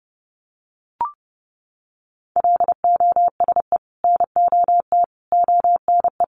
Morse code, heard as a keyed tone, about 30 wpm, 700 Hz. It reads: LOSE NOT ONE
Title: Sentences from Top 1100 Words 30wpm